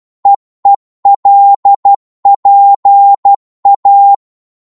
Morse code, heard as a keyed tone, about 12 wpm, 800 Hz. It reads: EELPA
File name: Morse / ltr-12wpm